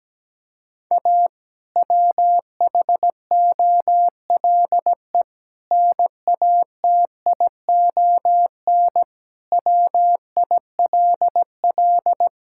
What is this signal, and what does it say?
Morse code, heard as a keyed tone, about 17 wpm, 700 Hz. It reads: A WHOLE NATION WILL
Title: Sentences from Top 1000 Words - Review Part 6 - 17wpm